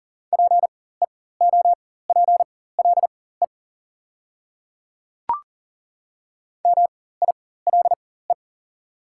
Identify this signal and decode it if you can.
Morse code, heard as a keyed tone, about 40 wpm, spaced as for 19 wpm, 700 Hz. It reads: PEOPLE MILE
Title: 4x Character Spacing - Top 300 Words - 40wpm